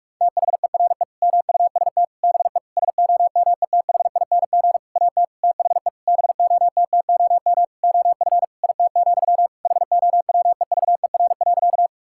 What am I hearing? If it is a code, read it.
Morse code, heard as a keyed tone, about 41 wpm, 700 Hz. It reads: THERE MUST BE SOMETHING AT THE BOTTOM OF IT, HOWEVER.
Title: Sentences from Top 1000 Words - Review Part 6 - 45wpm